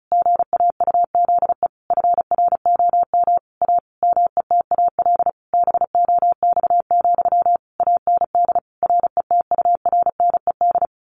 Morse code, heard as a keyed tone, about 34 wpm, 700 Hz. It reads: GAUZE FROM A METAL BOX, AND RETURNED
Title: A Sound Of Thunder 35wpm